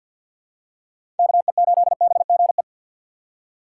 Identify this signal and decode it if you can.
Morse code, heard as a keyed tone, about 53 wpm, 700 Hz. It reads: KE8BGE
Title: USA Call Signs - 6 Characters - 6x Character Spacing - 50wpm